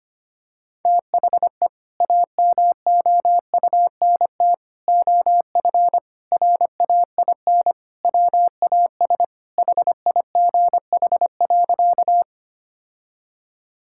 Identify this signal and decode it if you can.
Morse code, heard as a keyed tone, about 25 wpm, 700 Hz. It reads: THE AMOUNT OF RAIN WAS HIGH.
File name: practice / 25wpm / Sentences with 3-Word Phrases - 25wpm